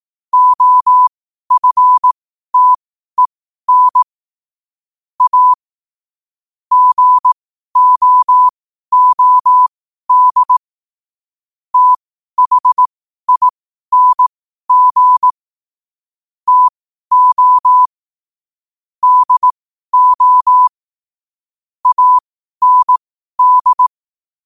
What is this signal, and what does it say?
Morse code, heard as a keyed tone, about 18 wpm, spaced as for 12 wpm, 1000 Hz. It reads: OFTEN A GOOD THING TO DO AND
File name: qotdfiles / 12wpm